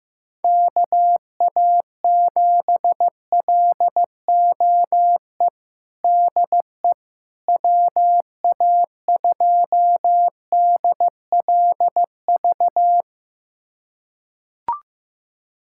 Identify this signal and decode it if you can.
Morse code, heard as a keyed tone, about 15 wpm, 700 Hz. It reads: KA7LOE DE WA2DLV
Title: USA Callsign DE Callsign - 6 Characters - 15wpm